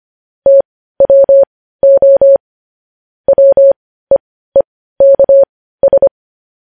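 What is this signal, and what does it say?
Morse code, heard as a keyed tone, about 25 wpm, spaced as for 15 wpm, 550 Hz. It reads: TWO WEEKS
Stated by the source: Morse news headlines sent at 15 WPM, beginning with vvv.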